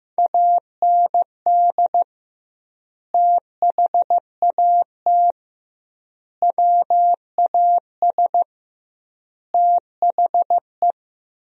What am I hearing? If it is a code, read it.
Morse code, heard as a keyed tone, about 16 wpm, 700 Hz. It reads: AND THAT WAS THE
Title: 2x Word-Spacing - Sentences from Top 200 Words - 15wpm